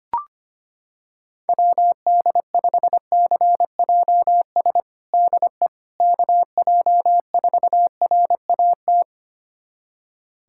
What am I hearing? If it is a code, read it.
Morse code, heard as a keyed tone, about 25 wpm, 700 Hz. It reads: WD5CJS DE KJ4RAT
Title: USA Callsign DE Callsign 25wpm